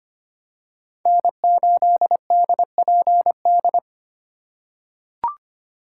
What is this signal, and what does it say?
Morse code, heard as a keyed tone, about 25 wpm, 700 Hz. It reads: N8DPD
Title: K1USN SST QSO Practice - 25wpm